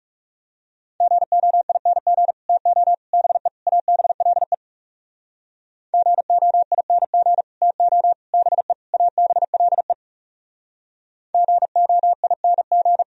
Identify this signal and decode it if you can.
Morse code, heard as a keyed tone, about 38 wpm, 700 Hz. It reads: GOING TO BE ABLE GOING TO BE ABLE GOING
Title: Speed-Racing - Sets of 4 Words 30wpm